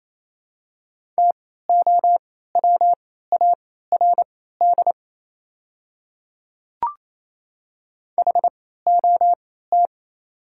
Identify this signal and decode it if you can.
Morse code, heard as a keyed tone, about 28 wpm, spaced as for 16 wpm, 700 Hz. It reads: TOWARD HOT
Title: 3x Character Spacing - Top 500 Words - 28wpm